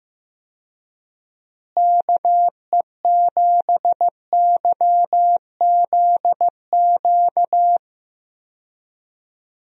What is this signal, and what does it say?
Morse code, heard as a keyed tone, about 15 wpm, 700 Hz. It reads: KE7YZQ
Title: USA Call Signs - 6 Characters - Repeated 2x - 15wpm